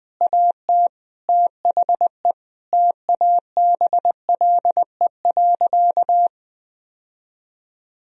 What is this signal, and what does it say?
Morse code, heard as a keyed tone, about 20 wpm, 700 Hz. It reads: AT THE TABLE.
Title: Sentences from Top 1000 Words - Review Part 2 - 20wpm